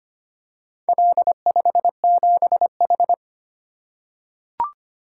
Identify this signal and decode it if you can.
Morse code, heard as a keyed tone, about 25 wpm, 700 Hz. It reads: L57H
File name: practice / 25wpm / Four Letter-Number Combinations - 25wpm